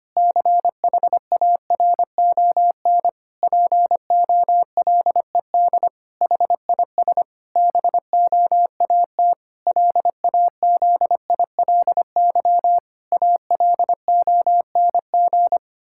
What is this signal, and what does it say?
Morse code, heard as a keyed tone, about 25 wpm, 700 Hz. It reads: CHARON POLED HIS BOAT LAZILY ALONG
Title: Pangram Challenge v2 - 25wpm